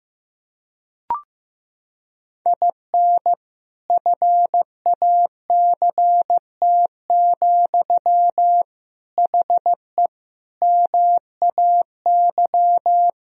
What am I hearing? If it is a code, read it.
Morse code, heard as a keyed tone, about 15 wpm, 700 Hz. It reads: IN FACT, HE MAY
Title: Sentences from Top 1000 Words - Review Part 3 - 15wpm